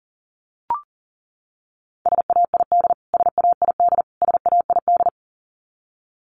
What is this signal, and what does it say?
Morse code, heard as a keyed tone, about 38 wpm, 700 Hz. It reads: SAID SAID SAID
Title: Top 500 Words Repeated 40wpm